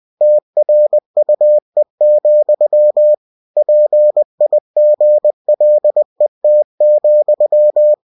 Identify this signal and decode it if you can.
Morse code, heard as a keyed tone, about 20 wpm, 600 Hz. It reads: TRUE, PIGLET,
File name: cw-books / winnie-the-pooh / winnie-20wpm-page-0034